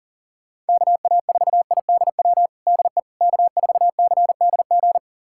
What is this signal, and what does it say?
Morse code, heard as a keyed tone, about 40 wpm, 700 Hz. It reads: KA4IDW DE K4CDG
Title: USA Callsign DE Callsign 40wpm